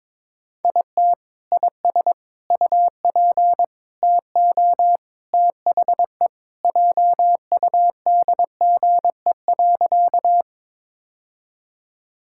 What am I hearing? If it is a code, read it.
Morse code, heard as a keyed tone, about 22 wpm, 700 Hz. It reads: IT IS UP TO THE JUDGE.
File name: practice / 22wpm / Sentences with 3-Word Phrases - 22wpm